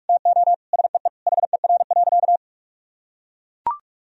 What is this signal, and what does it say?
Morse code, heard as a keyed tone, about 43 wpm, 700 Hz. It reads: TO SEE HER.
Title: Sentences from Top 1000 Words - Review Part 4 - 45wpm